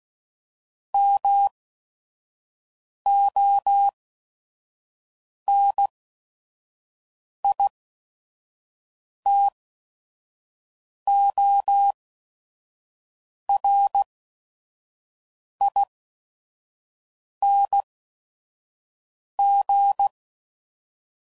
Morse code, heard as a keyed tone, about 16 wpm, spaced as for 5 wpm, 800 Hz. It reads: MONITORING
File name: Suffixes / Special ING's / VerySlowToFaster ING's